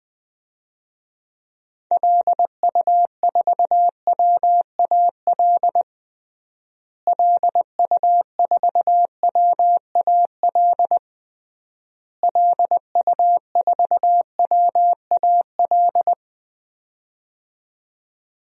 Morse code, heard as a keyed tone, about 20 wpm, 700 Hz. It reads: LU4WAL LU4WAL LU4WAL
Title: DXCC-100 Call Signs - 6 Characters - Repeated 3x - 20wpm